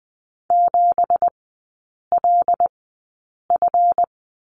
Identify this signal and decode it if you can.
Morse code, heard as a keyed tone, about 20 wpm, spaced as for 8 wpm, 700 Hz. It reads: 7LF
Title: 178 Lesson 29e - Slash - Three Characters 20wpm